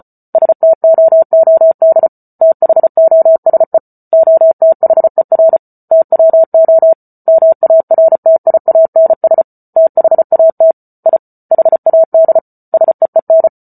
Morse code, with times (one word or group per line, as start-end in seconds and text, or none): 0.3-2.1 STOOD
2.4-3.8 THOSE
4.1-5.6 OTHER
5.9-6.9 TWO
7.3-9.4 MARTIANS
9.8-10.7 THAT
11.1-11.2 I
11.5-12.4 HAD
12.7-13.5 SEEN